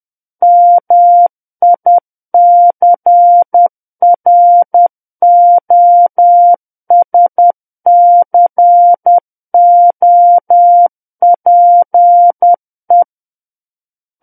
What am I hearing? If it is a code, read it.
Morse code, heard as a keyed tone, about 10 wpm, 700 Hz. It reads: MICROSCOPE
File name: Book One / 01-Chapter 1 - 10 WPM